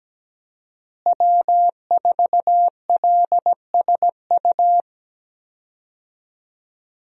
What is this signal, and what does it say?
Morse code, heard as a keyed tone, about 17 wpm, 700 Hz. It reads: W4LSU
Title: USA Call Signs - 5 Characters - Repeated 3x - 17wpm